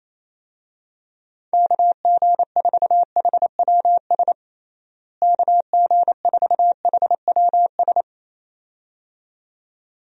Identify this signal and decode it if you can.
Morse code, heard as a keyed tone, about 27 wpm, 700 Hz. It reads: KG4HWS KG4HWS
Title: USA Call Signs - 6 Characters - Repeated 2x - 28wpm